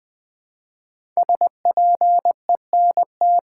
Morse code, heard as a keyed tone, about 20 wpm, 700 Hz. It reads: SPENT
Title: Build A Phrase - 5-Word Phrases - 20wpm